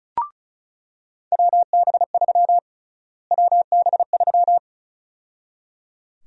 Morse code, heard as a keyed tone, about 36 wpm, 700 Hz. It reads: WB3 WB3
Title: USA Call Sign Prefixes - Repeated 2x - 35wpm